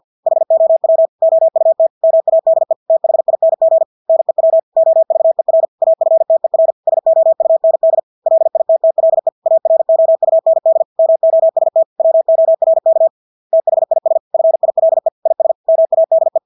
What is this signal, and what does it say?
Morse code, heard as a keyed tone, about 50 wpm, 650 Hz. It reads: HOW OUT MADE THING NEW OVER AFTER SOUND LITTLE AROUND MOST WORK THIS FILE IS MADE
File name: Engelske ord 50 wpm